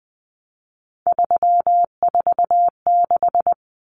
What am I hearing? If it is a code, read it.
Morse code, heard as a keyed tone, about 20 wpm, 700 Hz. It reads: 346
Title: Common Cut Numbers - 3-Digit 20wpm